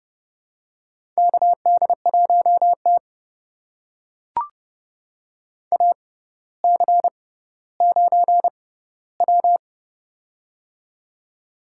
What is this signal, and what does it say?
Morse code, heard as a keyed tone, about 30 wpm, spaced as for 15 wpm, 700 Hz. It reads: KD1T AC9W
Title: USA Call Signs - 4 Characters - 6x Character Spacing - 30wpm